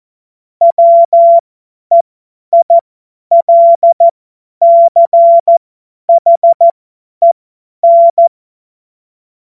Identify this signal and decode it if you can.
Morse code, heard as a keyed tone, about 14 wpm, spaced as for 10 wpm, 680 Hz. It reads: WEILCHEN
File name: Morsekurs / Witze / Witz 01. August 2023 50 F70